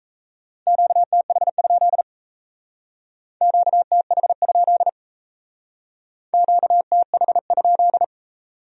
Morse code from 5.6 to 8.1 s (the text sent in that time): QTH?